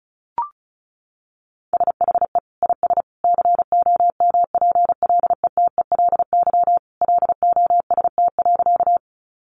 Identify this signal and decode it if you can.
Morse code, heard as a keyed tone, about 34 wpm, 700 Hz. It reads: SHE IS COMPLETELY LOST.
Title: Additional Easy Phrases 35wpm